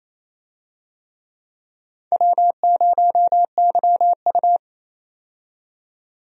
Morse code, heard as a keyed tone, about 27 wpm, 700 Hz. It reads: W0YU
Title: USA Call Signs - 4 Characters - 28wpm